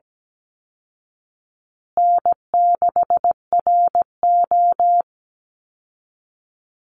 Morse code, heard as a keyed tone, about 17 wpm, 700 Hz. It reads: N6RO
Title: Easy Call Signs Repeated 3x - 17wpm